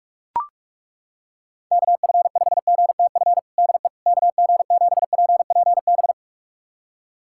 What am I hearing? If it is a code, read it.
Morse code, heard as a keyed tone, about 42 wpm, 700 Hz. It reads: KF5GTF DE KG7PPB
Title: USA Callsign DE Callsign 45wpm